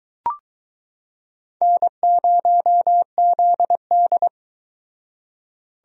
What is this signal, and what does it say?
Morse code, heard as a keyed tone, about 22 wpm, 700 Hz. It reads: N0ZD